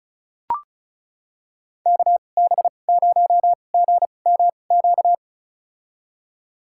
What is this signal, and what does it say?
Morse code, heard as a keyed tone, about 36 wpm, spaced as for 25 wpm, 700 Hz. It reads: KB0GMQ
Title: USA Call Signs - 2x Character Spacing - 35wpm